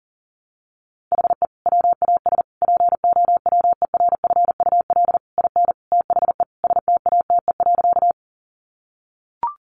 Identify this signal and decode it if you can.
Morse code, heard as a keyed tone, about 40 wpm, 700 Hz. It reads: HE WAS POWERFUL IN THE STATE.
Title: Sentences from Top 1300 Words 40wpm